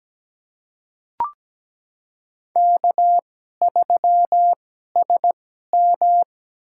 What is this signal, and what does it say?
Morse code, heard as a keyed tone, about 17 wpm, spaced as for 12 wpm, 700 Hz. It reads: K3SM
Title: USA Call Signs - 2x Character Spacing - 17wpm